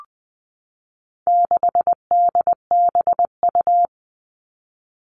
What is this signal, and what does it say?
Morse code, heard as a keyed tone, about 20 wpm, 700 Hz. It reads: 6DBU
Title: All Cut Numbers - 4-Digit 20wpm